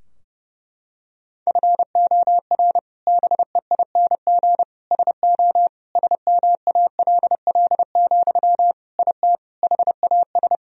Morse code, heard as a keyed tone, about 30 wpm, 700 Hz. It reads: FOR BEING SO SMALL, IT HAS
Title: Sentences from Top 1000 Words - Review Part 3 - 30wpm